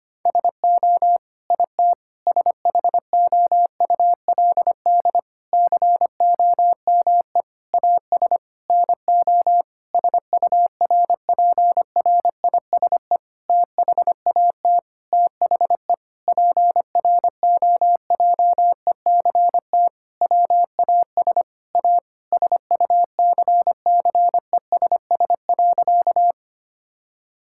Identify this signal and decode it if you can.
Morse code, heard as a keyed tone, about 25 wpm, 700 Hz. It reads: SO IT SHOULD COME AS NO SURPRISE THAT THE PROJECT WAS A SUCCESS.
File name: practice / 25wpm / Sentences with 9-Word Phrases - 25wpm